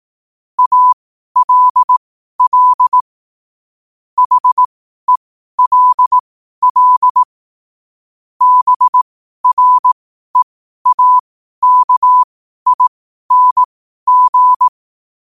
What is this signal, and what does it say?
Morse code, heard as a keyed tone, about 18 wpm, spaced as for 12 wpm, 1000 Hz. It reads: ALL HELL BREAKING
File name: qotdfiles / 12wpm